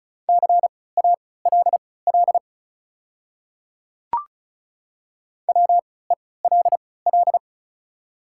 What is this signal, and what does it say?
Morse code, heard as a keyed tone, about 36 wpm, spaced as for 20 wpm, 690 Hz. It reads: CALL WELL
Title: 3x Character Spacing - Top 200 Words - 35wpm